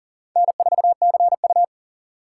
Morse code, heard as a keyed tone, about 39 wpm, 700 Hz. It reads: N4CU
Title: USA Call Signs - 4 Characters - 3x Character Spacing - 40wpm